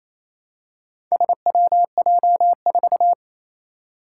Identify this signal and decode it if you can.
Morse code, heard as a keyed tone, about 27 wpm, 700 Hz. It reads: SWJ4
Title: Four Challenging Letter-Number Combinations 28wpm